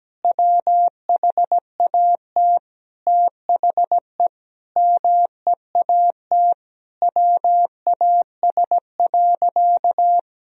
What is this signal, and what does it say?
Morse code, heard as a keyed tone, about 17 wpm, 700 Hz. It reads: WHAT THE MEAT WAS.
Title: Sentences from Top 1000 Words - Review Part 3 - 17wpm